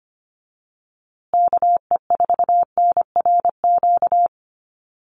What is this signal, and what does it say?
Morse code, heard as a keyed tone, about 25 wpm, 700 Hz. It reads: KE4NRQ
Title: USA Call Signs Repeated 2x 25wpm